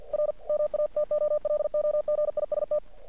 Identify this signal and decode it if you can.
Morse code, heard as a keyed tone, about 49 wpm, 600 Hz. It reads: RMATOLOGIST
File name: archived / CRSnet